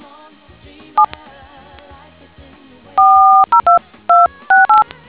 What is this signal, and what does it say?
DTMF, heard as telephone touch tones, tones of about 150 ms. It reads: 7 402268